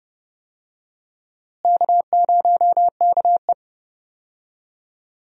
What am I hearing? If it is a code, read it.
Morse code, heard as a keyed tone, about 30 wpm, 700 Hz. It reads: K0KE
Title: USA Call Signs - 4 Characters - Repeated 2x - 30wpm